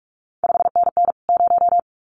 Morse code, heard as a keyed tone, about 46 wpm, 700 Hz. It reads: 5NN 0
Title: CQ WW DX Contest Exchange Encore - 45wpm